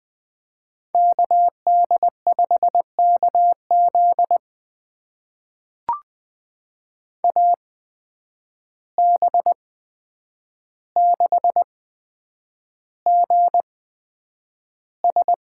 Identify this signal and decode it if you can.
Morse code, heard as a keyed tone, about 20 wpm, spaced as for 9 wpm, 700 Hz. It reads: KD5KZ AB6GS
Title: USA Call Signs - 5 Characters - 8x Character Spacing - 20wpm